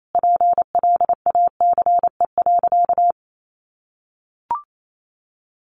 Morse code, heard as a keyed tone, about 28 wpm, 700 Hz. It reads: PLACE.
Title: Sentences from Top 1400 Words 28wpm